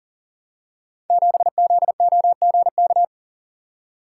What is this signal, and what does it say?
Morse code, heard as a keyed tone, about 40 wpm, 700 Hz. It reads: 7ZOGK
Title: Code Groups - 5 Letter:Numbers - 40wpm